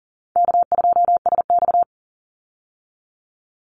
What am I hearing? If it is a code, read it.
Morse code, heard as a keyed tone, about 40 wpm, 700 Hz. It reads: K2SX
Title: Call Signs Repeated 2x - 40wpm